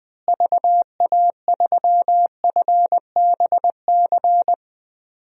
Morse code, heard as a keyed tone, about 20 wpm, 700 Hz. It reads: VA3FBC
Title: DXCC-100 Call Signs - 6 Characters - Repeated 2x - 20wpm